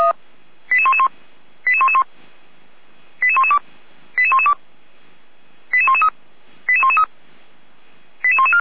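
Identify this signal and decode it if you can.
Five-tone selective call (ZVEI), heard as quick runs of five tones, tones of about 70 ms. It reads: 88111 88111 88112 88112 88113 88113 88114
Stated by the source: ZVEI five-tone selective call 88125 received on an SDR.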